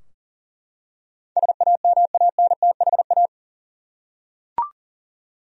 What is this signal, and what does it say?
Morse code, heard as a keyed tone, about 38 wpm, 700 Hz. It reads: SAMANTHA
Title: USA Common Names - 40wpm